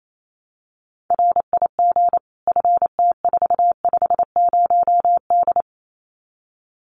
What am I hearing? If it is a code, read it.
Morse code, heard as a keyed tone, about 28 wpm, 700 Hz. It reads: RIG FT450D
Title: Common QSO Phrases 28wpm